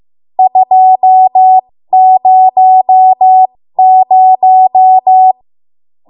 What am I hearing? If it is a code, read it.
Morse code, heard as a keyed tone, about 15 wpm, spaced as for 13 wpm, 750 Hz. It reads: 200
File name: Morse / 13WPM